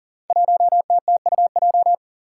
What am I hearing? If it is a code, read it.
Morse code, heard as a keyed tone, about 40 wpm, 700 Hz. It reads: 1TTUJ
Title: Five Challenging Letter-Number Combinations 40wpm